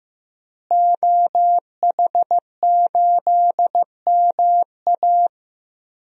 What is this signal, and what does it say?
Morse code, heard as a keyed tone, about 15 wpm, 700 Hz. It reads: OH8MA
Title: DXCC-100 Call Signs - 5 Characters - Repeated 3x - 15wpm